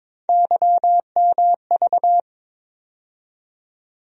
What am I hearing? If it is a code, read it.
Morse code, heard as a keyed tone, about 22 wpm, 700 Hz. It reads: YMV
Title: USA Call Sign Postfixes - 3 Letters 22wpm